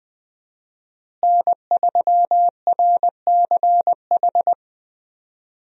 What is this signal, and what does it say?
Morse code, heard as a keyed tone, about 20 wpm, 700 Hz. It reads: N3RCH
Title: USA Call Signs - 5 Characters - Repeated 3x - 20wpm